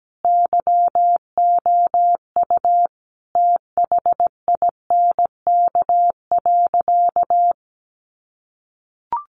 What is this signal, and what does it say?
Morse code, heard as a keyed tone, about 17 wpm, 700 Hz. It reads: YOU THINK.
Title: Sentences from Top 400 Words 17wpm